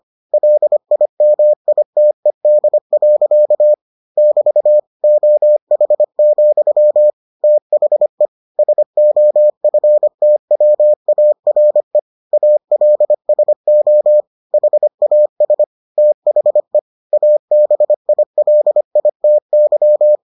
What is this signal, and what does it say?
Morse code, heard as a keyed tone, about 25 wpm, 600 Hz. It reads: LIMITED. <BT> OH, THE SOFTWARE ALSO HAS THE ABILITY